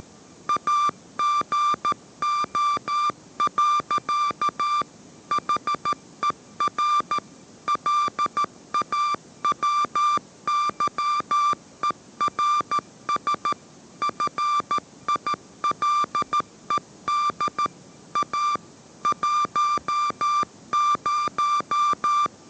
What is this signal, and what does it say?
Morse code, heard as a keyed tone, about 13 wpm, 1180 Hz. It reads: AGO. HER LAWYERS FILED A 10